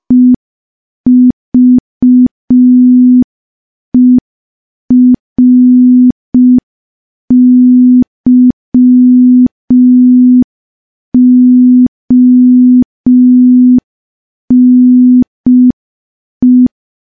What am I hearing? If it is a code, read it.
Morse code, heard as a keyed tone, about 5 wpm, 250 Hz. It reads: EVERYONE
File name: morse / morse-training-05WPM